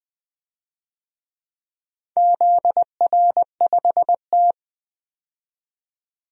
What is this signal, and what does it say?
Morse code, heard as a keyed tone, about 20 wpm, 700 Hz. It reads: ZR5T
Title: DXCC-100 Call Signs - 4 Characters - 20wpm